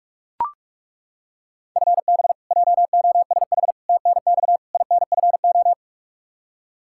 Morse code, heard as a keyed tone, about 41 wpm, 700 Hz. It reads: FB JOSH TNX INFO
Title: Practice QSO - Exchange 3 - More Info - 45wpm